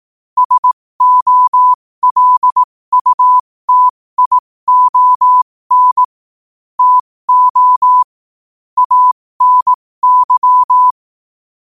Morse code, heard as a keyed tone, about 18 wpm, spaced as for 15 wpm, 1000 Hz. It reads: SOLUTION TO ANY
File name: qotdfiles / 15wpm